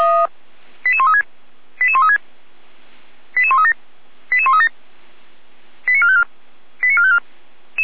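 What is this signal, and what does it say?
Five-tone selective call (ZVEI), heard as quick runs of five tones, tones of about 70 ms. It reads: 88136 88136 88137 88137 89454 89454
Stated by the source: ZVEI five-tone call received by SDR, recorded by Feuerwehr Orlamünde.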